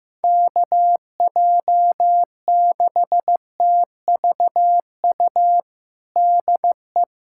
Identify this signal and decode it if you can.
Morse code, heard as a keyed tone, about 15 wpm, 700 Hz. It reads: KJ6TVU DE
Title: USA Callsign DE Callsign 15wpm